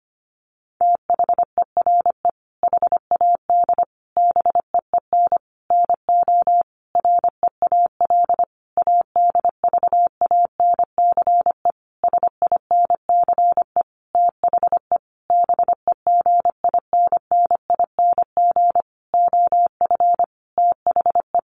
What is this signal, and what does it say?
Morse code, heard as a keyed tone, about 25 wpm, 700 Hz. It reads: THERE HAD BEEN NO REAL ADVANCE SINCE THE BEGINNING OF THE
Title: Sentences from Top 1400 Words 25wpm